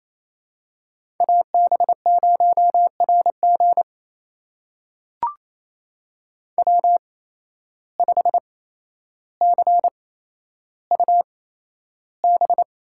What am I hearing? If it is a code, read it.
Morse code, heard as a keyed tone, about 28 wpm, spaced as for 12 wpm, 700 Hz. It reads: AB0RG W5CUB
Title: USA Call Signs - 5 Characters - 8x Character Spacing - 28wpm